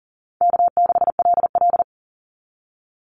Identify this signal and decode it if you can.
Morse code, heard as a keyed tone, about 42 wpm, 700 Hz. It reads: K6LL
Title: Easy Call Signs - 40wpm